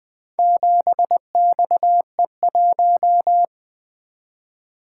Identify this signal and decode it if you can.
Morse code, heard as a keyed tone, about 20 wpm, 700 Hz. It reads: 7XE1
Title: Four Letter-Number Combinations - 20wpm